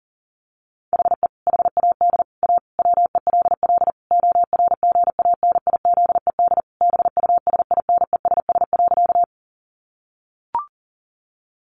Morse code, heard as a keyed tone, about 40 wpm, 700 Hz. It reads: HE HAD A WELL ORGANIZED BUSINESS.
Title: Sentences from Top 1300 Words 40wpm